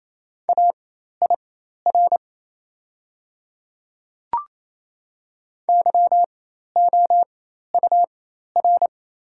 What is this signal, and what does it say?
Morse code, heard as a keyed tone, about 28 wpm, spaced as for 13 wpm, 690 Hz. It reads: AIR YOUR
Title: 4x Character Spacing - Top 200 Words - 28wpm